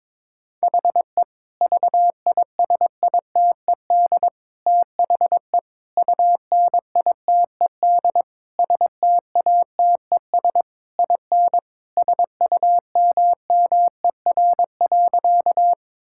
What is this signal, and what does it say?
Morse code, heard as a keyed tone, about 22 wpm, 700 Hz. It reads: HE VISITED THE UNITED STATES IN SUMMER.
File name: practice / 22wpm / Sentences with 4-Word Phrases - 22wpm